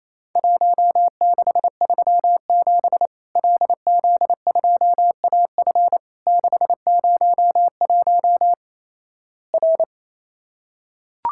Morse code, heard as a keyed tone, about 28 wpm, 700 Hz. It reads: 1637 LZ2AF 601 R
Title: Worked All Europe DX Contest - QTC Exchange - 28wpm